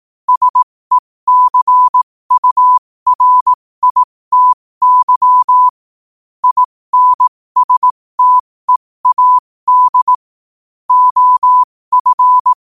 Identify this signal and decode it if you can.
Morse code, heard as a keyed tone, about 18 wpm, spaced as for 15 wpm, 1000 Hz. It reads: SECURITY INSTEAD OF
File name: qotdfiles / 15wpm